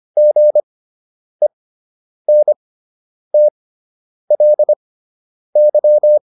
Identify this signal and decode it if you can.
Morse code, heard as a keyed tone, about 25 wpm, spaced as for 9 wpm, 600 Hz. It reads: GENTLY